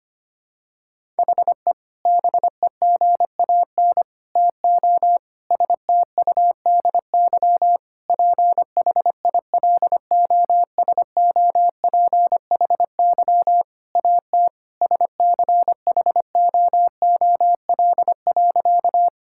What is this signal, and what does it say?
Morse code, heard as a keyed tone, about 25 wpm, 700 Hz. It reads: HE BEGAN TO STUDY PHILOSOPHY AT SCHOOL.
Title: Sentences from Top 1000 Words - Review Part 6 - 25wpm